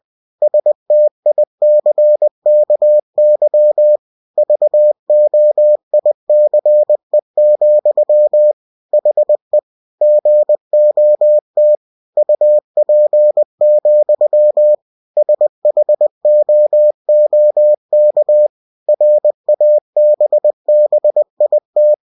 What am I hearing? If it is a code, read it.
Morse code, heard as a keyed tone, about 20 wpm, 600 Hz. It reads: STICKY VOICE, HE GOT UP, SHOOK RABBIT
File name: cw-books / winnie-the-pooh / winnie-20wpm-page-0015